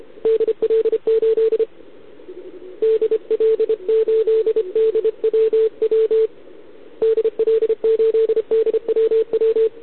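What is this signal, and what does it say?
Morse code, heard as a keyed tone, about 28 wpm, 440 Hz, with radio noise. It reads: DL8 DL8DWW DL8DWW
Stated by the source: Contest CW from OK1RI on 80 m during the OKOM DX contest, recorded by a QRP station using a KX3 and a wire antenna.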